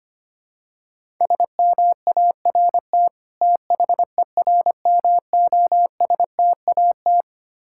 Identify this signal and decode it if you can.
Morse code, heard as a keyed tone, about 25 wpm, 700 Hz. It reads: SMART THERMOSTAT
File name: practice / 25wpm / Difficult Words with Hints - 25wpm